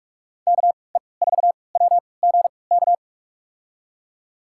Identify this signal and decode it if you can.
Morse code, heard as a keyed tone, about 47 wpm, spaced as for 26 wpm, 700 Hz. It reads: KE4WGK
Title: USA Call Signs - 6 Characters - 3x Character Spacing - 45wpm